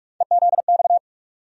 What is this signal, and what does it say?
Morse code, heard as a keyed tone, about 41 wpm, 700 Hz. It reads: EZX
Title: USA Call Sign Postfixes - 3 Letters 45wpm